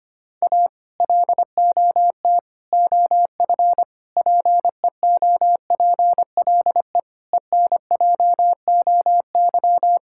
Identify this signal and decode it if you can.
Morse code, heard as a keyed tone, about 25 wpm, 700 Hz. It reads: A LOT OF PEOPLE ENJOY
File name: practice / 25wpm / Sentences with 4-Word Phrases - 25wpm